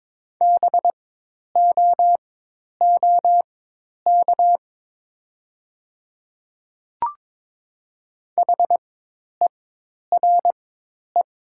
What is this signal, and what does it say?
Morse code, heard as a keyed tone, about 22 wpm, spaced as for 10 wpm, 700 Hz. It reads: BOOK HERE
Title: 4x Character Spacing - Top 300 Words - 22wpm